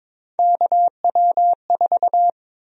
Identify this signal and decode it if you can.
Morse code, heard as a keyed tone, about 22 wpm, 700 Hz. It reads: KW4
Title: USA Call Sign Prefixes - Repeated 2x - 22wpm